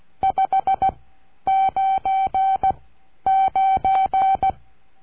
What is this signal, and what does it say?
Morse code, heard as a keyed tone, about 16 wpm, spaced as for 10 wpm, 770 Hz. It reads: 599
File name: cursos / CURSO DE TELEGRAFIA / Telegrafia - Trilha 22